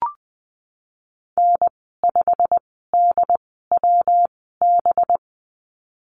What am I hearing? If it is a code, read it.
Morse code, heard as a keyed tone, about 20 wpm, spaced as for 15 wpm, 700 Hz. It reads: N5DWB
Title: 193 Lesson 31g - Letter v - Call Signs 20wpm